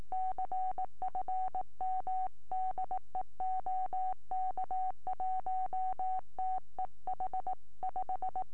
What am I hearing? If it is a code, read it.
Morse code, heard as a keyed tone, about 18 wpm, 750 Hz. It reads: CFM DE OK1TEH 5